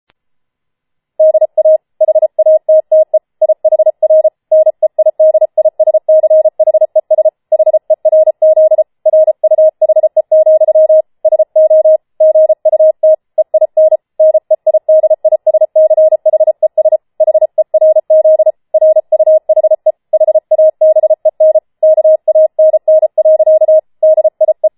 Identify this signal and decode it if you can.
Morse code, heard as a keyed tone, about 33 wpm, 630 Hz. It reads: DA HATTE IHR NEIDISCHES HERZ RUHE, SO GUT EIN NEIDISCHES HERZ RUHE HABEN KANN. DIE
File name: Schneewittchen Teil5 34wpm